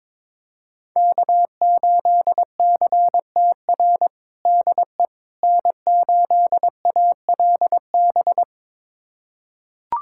Morse code, heard as a keyed tone, about 22 wpm, 700 Hz. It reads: K8CTR DE N8ALB
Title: USA Callsign DE Callsign - 5 Characters - 22wpm